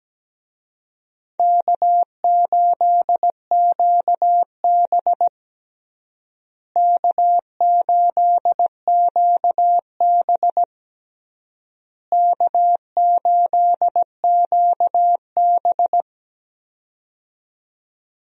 Morse code, heard as a keyed tone, about 17 wpm, 700 Hz. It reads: K8QB K8QB K8QB
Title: USA Call Signs - 4 Characters - Repeated 3x - 17wpm